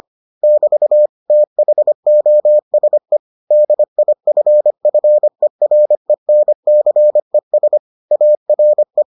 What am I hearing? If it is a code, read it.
Morse code, heard as a keyed tone, about 25 wpm, 600 Hz. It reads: <BT> THOSE DIFFERENCES ARE